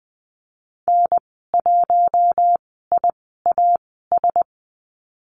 Morse code, heard as a keyed tone, about 20 wpm, spaced as for 15 wpm, 700 Hz. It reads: N1IAS
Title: 048 Lesson 9g - Number 4 - Call Signs 20wpm